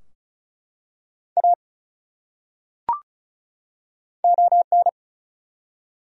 Morse code, heard as a keyed tone, about 34 wpm, 700 Hz. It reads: A ON
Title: Top 100 Words 35wpm